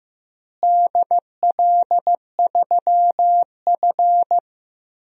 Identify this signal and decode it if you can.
Morse code, heard as a keyed tone, about 15 wpm, 700 Hz. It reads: DL3F
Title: DXCC-100 Call Signs - 4 Characters - Repeated 3x - 15wpm